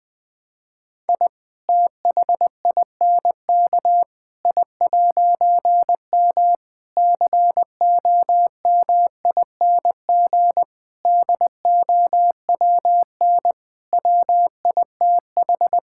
Morse code, heard as a keyed tone, about 21 wpm, 700 Hz. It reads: I THINK I'M COMING DOWN WITH
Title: All-Age Appropriate Jokes - 20wpm